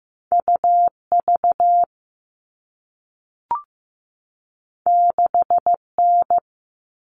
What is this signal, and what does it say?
Morse code, heard as a keyed tone, about 15 wpm, 700 Hz. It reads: UV 6N
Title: All Cut Numbers - 2-Digit 15wpm